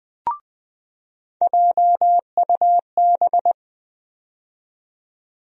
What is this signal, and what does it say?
Morse code, heard as a keyed tone, about 20 wpm, 700 Hz. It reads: JUB
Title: USA Call Sign Postfixes - 3 Letters 20wpm